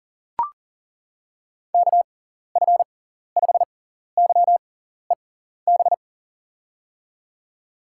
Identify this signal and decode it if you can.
Morse code, heard as a keyed tone, about 37 wpm, spaced as for 14 wpm, 700 Hz. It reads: KF5YEB
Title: USA Call Signs - 6 Characters - 6x Character Spacing - 40wpm